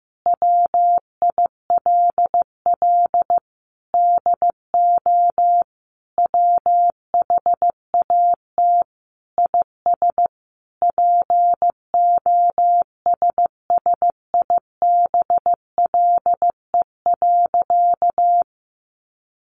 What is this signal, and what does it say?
Morse code, heard as a keyed tone, about 15 wpm, 700 Hz. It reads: WILL DO WHAT IS POSSIBLE.
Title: Sentences from Top 900 Words 15wpm